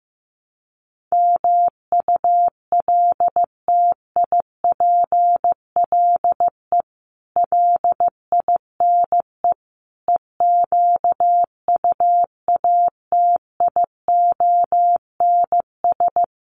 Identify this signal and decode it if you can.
Morse code, heard as a keyed tone, about 15 wpm, 700 Hz. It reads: MULTIPLE LINE EQUATIONS
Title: Sets of 5 Words 15wpm